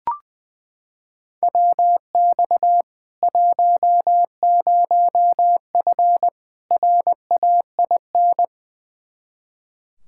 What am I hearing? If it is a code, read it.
Morse code, heard as a keyed tone, about 21 wpm, 700 Hz. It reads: WX 10F RAIN
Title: K1USN SST QSO Practice 20wpm